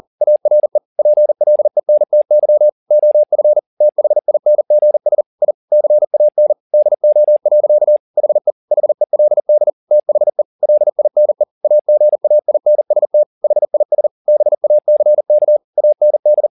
Morse code, heard as a keyed tone, about 40 wpm, 600 Hz. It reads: ARE PLENTY OF THINGS I CAN DO. HE HELD THE LINE AGAINST HIS BACK AND